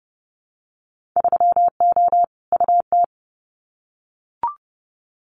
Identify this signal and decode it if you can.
Morse code, heard as a keyed tone, about 30 wpm, 700 Hz. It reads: 3O UT
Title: Winter Field Day Exchanges 30wpm